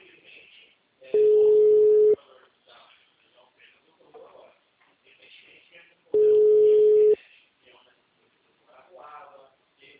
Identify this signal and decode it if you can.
Ringback tone (Europe, 425 Hz), with speech.